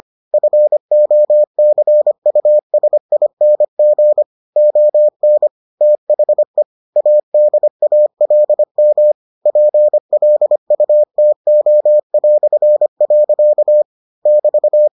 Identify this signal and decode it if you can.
Morse code, heard as a keyed tone, about 25 wpm, 600 Hz. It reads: FOCUSING ON THE ADALM PLUTO". <BT>